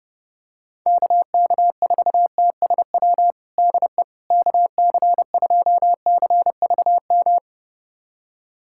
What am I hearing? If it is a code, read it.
Morse code, heard as a keyed tone, about 30 wpm, 700 Hz. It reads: KK4TSW DE KC2CVM
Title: USA Callsign DE Callsign - 6 Characters - 30wpm